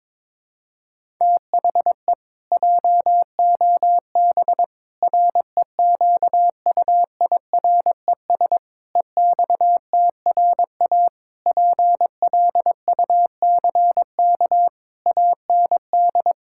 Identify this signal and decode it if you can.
Morse code, heard as a keyed tone, about 22 wpm, 700 Hz. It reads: THE JOB REQUIRES EXTRA PLUCK AND
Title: Pangram Challenge v2 - 22wpm